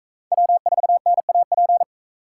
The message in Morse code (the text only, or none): W4NAP